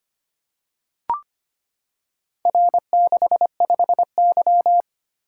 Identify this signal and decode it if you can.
Morse code, heard as a keyed tone, about 25 wpm, 690 Hz. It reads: R65Y
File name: practice / 25wpm / Four Letter-Number Combinations - 25wpm